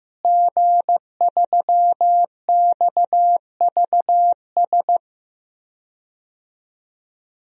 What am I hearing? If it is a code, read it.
Morse code, heard as a keyed tone, about 15 wpm, 700 Hz. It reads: G3XVS
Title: DXCC-100 Call Signs - 5 Characters - Repeated 2x - 15wpm